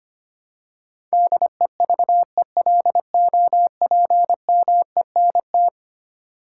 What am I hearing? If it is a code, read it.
Morse code, heard as a keyed tone, about 25 wpm, 700 Hz. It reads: DEVELOPMENT
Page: morse-code-ninja-practice-25wpm